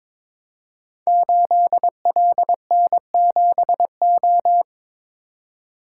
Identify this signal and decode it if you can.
Morse code, heard as a keyed tone, about 22 wpm, 700 Hz. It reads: 8LN7O
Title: Code Groups - 5 Letter:Numbers - 22wpm